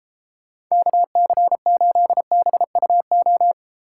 Morse code, heard as a keyed tone, about 32 wpm, 700 Hz. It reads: KC8BUO
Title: Speed Racing - USA Call Signs 28wpm